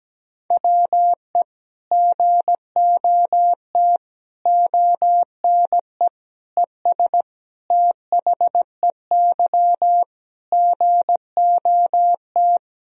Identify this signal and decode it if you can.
Morse code, heard as a keyed tone, about 17 wpm, 700 Hz. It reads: WE GOT ONE ES THEY GOT
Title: Sentences from Top 300 Words and Abbreviations 17wpm